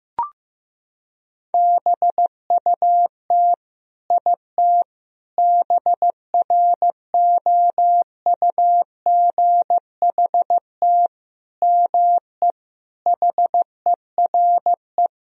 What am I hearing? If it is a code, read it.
Morse code, heard as a keyed tone, about 15 wpm, 700 Hz. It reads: BUT IT BROUGHT ME HERE
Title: Sentences from Top 1000 Words - Review Part 4 - 15wpm